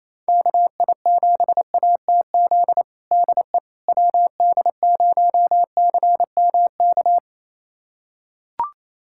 Morse code, heard as a keyed tone, about 27 wpm, 700 Hz. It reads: KI7ATZ DE WD0CMK
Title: USA Callsign DE Callsign - 6 Characters - 28wpm